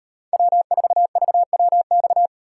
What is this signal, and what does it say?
Morse code, heard as a keyed tone, about 32 wpm, 700 Hz. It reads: W4VWX